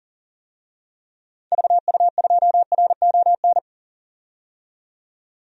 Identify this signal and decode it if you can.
Morse code, heard as a keyed tone, about 37 wpm, 700 Hz. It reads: VU2RON
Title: DXCC-100 Call Signs - 6 Characters - 40wpm